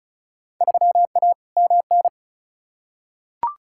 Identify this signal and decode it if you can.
Morse code, heard as a keyed tone, about 34 wpm, 700 Hz. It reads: 3A MN
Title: Field Day Exchanges - 35wpm